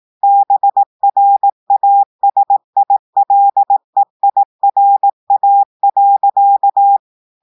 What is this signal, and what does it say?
Morse code, heard as a keyed tone, about 18 wpm, 800 Hz. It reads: BRASILEIRA.
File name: audios / portugais / portugais texte en clair 1 vitesse 18 mots minutes